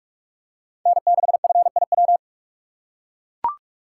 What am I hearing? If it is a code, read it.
Morse code, heard as a keyed tone, about 41 wpm, 700 Hz. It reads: N6FIW